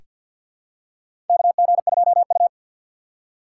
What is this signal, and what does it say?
Morse code, heard as a keyed tone, about 47 wpm, 700 Hz. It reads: KG2U
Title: USA Call Signs - 4 Characters - 2x Character Spacing - 50wpm